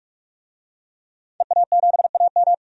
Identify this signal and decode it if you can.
Morse code, heard as a keyed tone, about 42 wpm, 700 Hz. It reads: EA7AM
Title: DXCC-100 Call Signs - 5 Characters - Repeated 2x - 45wpm